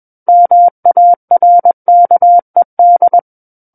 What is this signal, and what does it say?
Morse code, heard as a keyed tone, about 21 wpm, 700 Hz. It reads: MARKED
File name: Book One / 12-Chapter 12 - 21 WPM